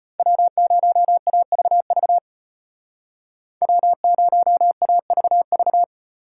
Morse code, heard as a keyed tone, about 33 wpm, 700 Hz. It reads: W0AVV W0AVV